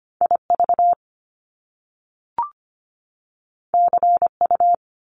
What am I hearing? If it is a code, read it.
Morse code, heard as a keyed tone, about 25 wpm, 700 Hz. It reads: IV CU
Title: Top Two Letter Combinations in Words 25wpm